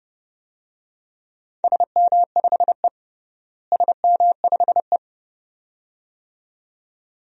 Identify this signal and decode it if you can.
Morse code, heard as a keyed tone, about 30 wpm, 700 Hz. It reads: SM5E SM5E
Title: DXCC-100 Call Signs - 4 Characters - Repeated 2x - 30wpm